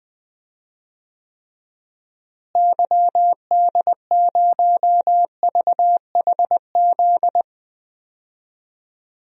Morse code, heard as a keyed tone, about 20 wpm, 700 Hz. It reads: YD0VHZ
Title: DXCC-100 Call Signs - 6 Characters - Repeated 2x - 20wpm